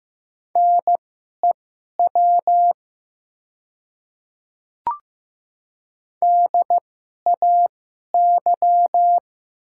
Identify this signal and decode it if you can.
Morse code, heard as a keyed tone, about 15 wpm, spaced as for 11 wpm, 700 Hz. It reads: NEW DAY
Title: 2x Character Spacing - Top 300 Words - 15wpm